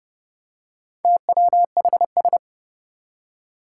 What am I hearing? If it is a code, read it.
Morse code, heard as a keyed tone, about 30 wpm, 700 Hz. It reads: TWHS
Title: Four Challenging Letter-Number Combinations 30wpm